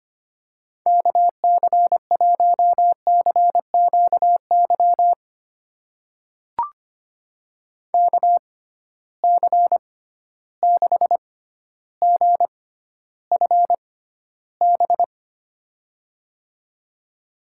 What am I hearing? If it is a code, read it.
Morse code, heard as a keyed tone, about 25 wpm, spaced as for 13 wpm, 700 Hz. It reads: KC1CQY KC6GFB
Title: USA Call Signs - 6x Character Spacing - 25wpm